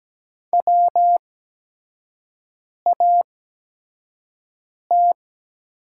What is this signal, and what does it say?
Morse code, heard as a keyed tone, about 17 wpm, spaced as for 5 wpm, 700 Hz. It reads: WAT
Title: 8x Character Spacing - Top 300 Words - 17wpm